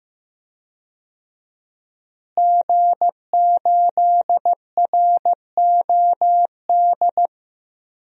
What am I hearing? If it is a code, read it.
Morse code, heard as a keyed tone, about 15 wpm, 700 Hz. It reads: G8ROD
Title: DXCC-100 Call Signs - 5 Characters - Repeated 3x - 15wpm